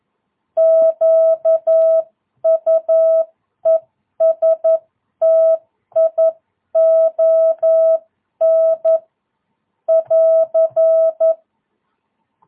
Morse code, heard as a keyed tone, about 11 wpm, 660 Hz. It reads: QUESTION <AR>
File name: jh8cbh / 2022cw